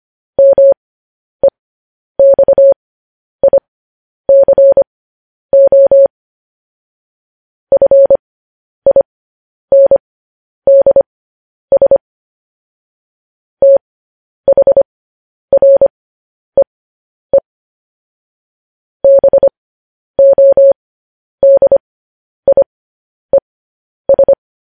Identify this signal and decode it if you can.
Morse code, heard as a keyed tone, about 25 wpm, spaced as for 10 wpm, 550 Hz. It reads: MEXICO FINDS THREE BODIES
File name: podcast / News headlines-1x-25x10